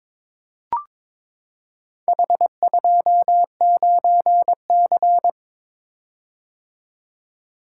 Morse code, heard as a keyed tone, about 22 wpm, 700 Hz. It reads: H29C
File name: practice / 22wpm / Four Letter-Number Combinations - 22wpm